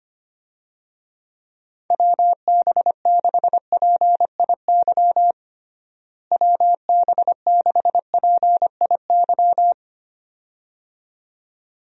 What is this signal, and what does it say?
Morse code, heard as a keyed tone, about 25 wpm, 700 Hz. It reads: WB6PIY WB6PIY
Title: USA Call Signs - 6 Characters - Repeated 2x - 25wpm